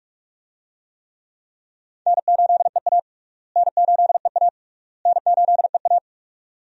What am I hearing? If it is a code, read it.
Morse code, heard as a keyed tone, about 43 wpm, 700 Hz. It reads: N8EA N8EA N8EA
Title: USA Call Signs - 4 Characters - Repeated 3x - 45wpm